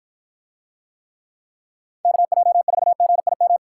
Morse code, heard as a keyed tone, about 53 wpm, 700 Hz. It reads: KJ4GIM